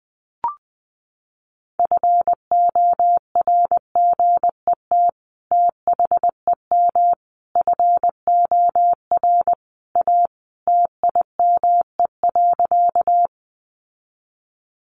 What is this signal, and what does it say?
Morse code, heard as a keyed tone, about 20 wpm, 700 Hz. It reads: FORGET THEM FOR A TIME.
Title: Sentences from Top 1200 Words 20wpm